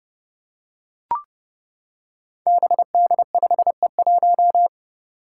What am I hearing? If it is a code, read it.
Morse code, heard as a keyed tone, about 30 wpm, 700 Hz. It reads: BD5E1
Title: Five Challenging Letter-Number Combinations 30wpm